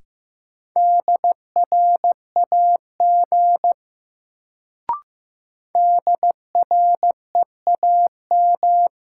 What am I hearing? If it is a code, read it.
Morse code, heard as a keyed tone, about 15 wpm, 700 Hz. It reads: DRAG DREAM
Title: Sentences from Top 1500 Words - New Word Drill Alt - 15wpm